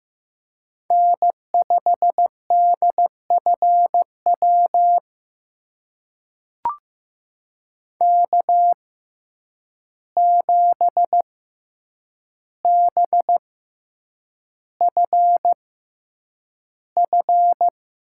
Morse code, heard as a keyed tone, about 15 wpm, spaced as for 8 wpm, 700 Hz. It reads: N5DFW K7BFF
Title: USA Call Signs - 5 Characters - 6x Character Spacing - 15wpm